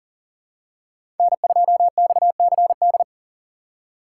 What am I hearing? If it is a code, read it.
Morse code, heard as a keyed tone, about 39 wpm, 700 Hz. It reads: N2XCD
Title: USA Call Signs - 5 Characters - 4x Character Spacing - 40wpm